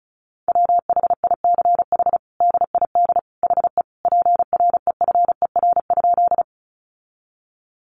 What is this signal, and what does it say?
Morse code, heard as a keyed tone, about 34 wpm, 700 Hz. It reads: WHICH DID HE PREFER?
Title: Sentences from Top 1100 Words 35wpm